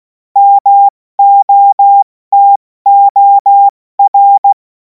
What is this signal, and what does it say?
Morse code, heard as a keyed tone, about 16 wpm, 800 Hz. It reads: MOTOR
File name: Morsekurs / Witze / Witz 28. April 2026 70 F80